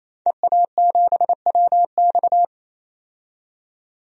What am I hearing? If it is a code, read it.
Morse code, heard as a keyed tone, about 28 wpm, 700 Hz. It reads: EA7WX
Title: DXCC-100 Call Signs - 5 Characters - 28wpm